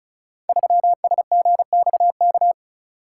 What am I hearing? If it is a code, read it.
Morse code, heard as a keyed tone, about 36 wpm, 700 Hz. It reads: 3SGXK
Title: Code Groups - 5 Letter:Numbers - 35wpm